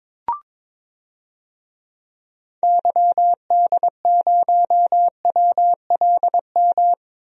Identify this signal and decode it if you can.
Morse code, heard as a keyed tone, about 22 wpm, 700 Hz. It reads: YD0WLM
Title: DXCC-100 Call Signs - 6 Characters - 22wpm